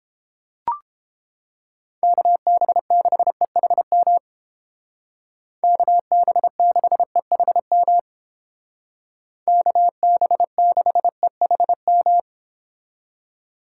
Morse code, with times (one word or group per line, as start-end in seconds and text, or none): 2.0-4.2 KB6EHM
5.6-8.0 KB6EHM
9.5-12.2 KB6EHM